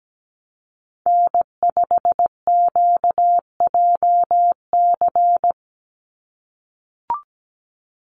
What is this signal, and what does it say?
Morse code, heard as a keyed tone, about 17 wpm, 700 Hz. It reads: N5QJC
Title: USA Call Signs 17wpm